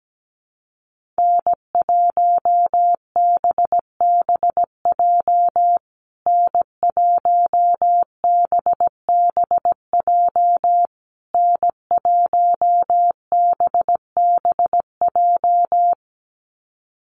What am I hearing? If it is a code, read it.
Morse code, heard as a keyed tone, about 17 wpm, 700 Hz. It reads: N1BBJ N1BBJ N1BBJ
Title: USA Call Signs Repeated 3x 17wpm